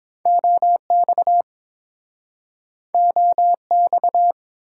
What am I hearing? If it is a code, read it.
Morse code, heard as a keyed tone, about 30 wpm, 700 Hz. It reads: OX OX